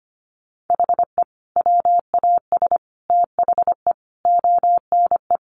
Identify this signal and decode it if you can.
Morse code, heard as a keyed tone, about 25 wpm, 700 Hz. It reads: HE WAS THE ONE
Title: Sets of 5 Words Encore2 25wpm